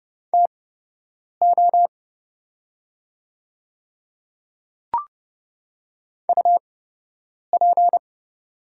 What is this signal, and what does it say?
Morse code, heard as a keyed tone, about 32 wpm, spaced as for 8 wpm, 700 Hz. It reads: TO UP
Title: 8x Character Spacing - Top 100 Words - 30wpm